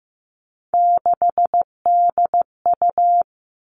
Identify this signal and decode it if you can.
Morse code, heard as a keyed tone, about 15 wpm, 700 Hz. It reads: 6DU
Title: Three Challenging Letter-Number Combinations 15wpm